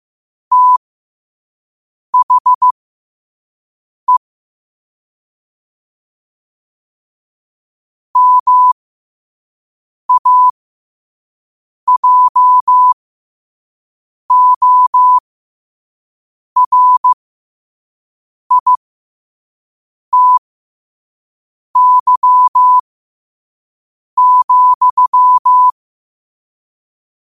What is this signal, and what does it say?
Morse code, heard as a keyed tone, about 15 wpm, spaced as for 5 wpm, 1000 Hz. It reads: THE MAJORITY,
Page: New quotes every day in morse code at 5 Words per minute.